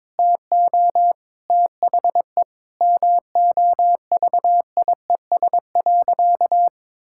Morse code, heard as a keyed tone, about 22 wpm, 700 Hz. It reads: TO THE MOVIES.
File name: practice / 22wpm / Sentences with 4-Word Phrases - 22wpm